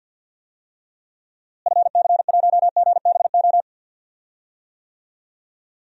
Morse code, heard as a keyed tone, about 49 wpm, 700 Hz. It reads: FC1GBO
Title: DXCC-100 Call Signs - 6 Characters - Repeated 2x - 50wpm